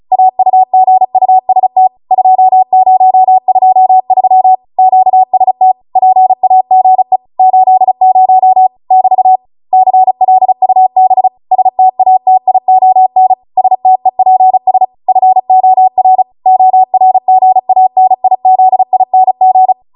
Morse code, heard as a keyed tone, about 36 wpm, 750 Hz. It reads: AUGUST 2023 QST PAGE 80 <BT> CLUB STATION STEPS FOR ORGANIZING